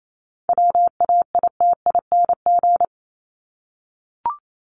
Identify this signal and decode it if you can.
Morse code, heard as a keyed tone, about 28 wpm, 700 Hz. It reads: WAITING
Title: ING Suffix - Top 100 Words with Base Words 28wpm